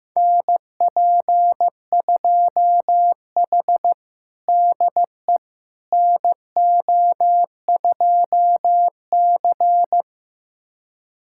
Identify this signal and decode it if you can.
Morse code, heard as a keyed tone, about 16 wpm, 700 Hz. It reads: NP2H DE NO2C
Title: USA Callsign DE Callsign - 4 Characters - 15wpm